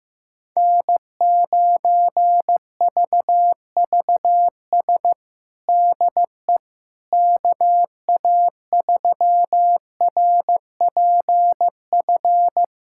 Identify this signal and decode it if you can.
Morse code, heard as a keyed tone, about 16 wpm, 700 Hz. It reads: N9VVS DE KA3RPF
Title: USA Callsign DE Callsign 15wpm